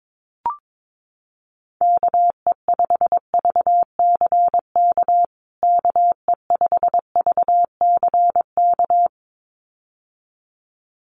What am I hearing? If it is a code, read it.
Morse code, heard as a keyed tone, about 22 wpm, 700 Hz. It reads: KE5VCK KE5VCK
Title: USA Call Signs Repeated 2x 22wpm